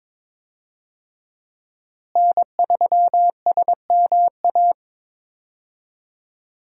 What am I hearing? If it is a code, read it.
Morse code, heard as a keyed tone, about 22 wpm, 700 Hz. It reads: N3SMA
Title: USA Call Signs - 5 Characters - Repeated 2x - 22wpm